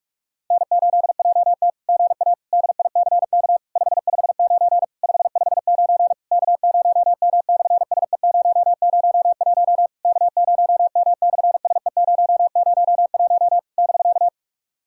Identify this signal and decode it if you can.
Morse code, heard as a keyed tone, about 42 wpm, 700 Hz. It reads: N8JT GA DICK 559 559 K0M/SE001 K0M/SE001 <BK>
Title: SOTA QSO Practice 45wpm